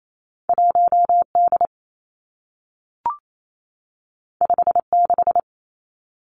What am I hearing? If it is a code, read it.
Morse code, heard as a keyed tone, about 28 wpm, 700 Hz. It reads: 1 D 5 6
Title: Two Challenging Letter-Number Combinations 28wpm